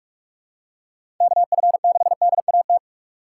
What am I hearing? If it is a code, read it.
Morse code, heard as a keyed tone, about 44 wpm, 700 Hz. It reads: KF6DAT
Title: USA Call Signs - 6 Characters - 3x Character Spacing - 45wpm